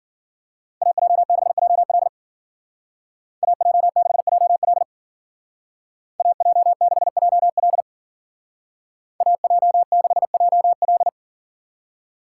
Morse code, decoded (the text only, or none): AJ6JL AJ6JL AJ6JL AJ6JL